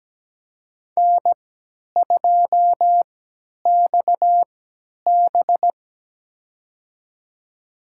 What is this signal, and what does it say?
Morse code, heard as a keyed tone, about 17 wpm, spaced as for 10 wpm, 700 Hz. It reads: N2XB
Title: USA Call Signs - 3x Character Spacing - 17wpm